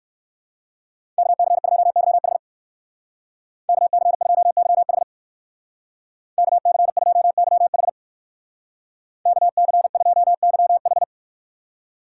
Morse code, heard as a keyed tone, about 55 wpm, 700 Hz. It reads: KC2YH KC2YH KC2YH KC2YH
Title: Speed Racing - USA 2x2 Call Signs 45wpm